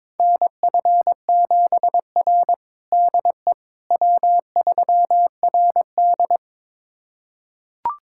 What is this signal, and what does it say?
Morse code, heard as a keyed tone, about 21 wpm, 700 Hz. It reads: NF7R DE W3RD
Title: USA Callsign DE Callsign - 4 Characters - 22wpm